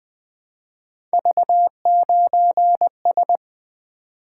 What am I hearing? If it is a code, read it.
Morse code, heard as a keyed tone, about 20 wpm, 700 Hz. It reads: V9S
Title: Three Letter-Number Combinations - 20wpm